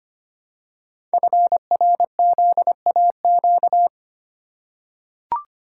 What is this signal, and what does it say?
Morse code, heard as a keyed tone, about 25 wpm, 700 Hz. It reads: FRZAQ
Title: Code Groups - 5 Letters - 25wpm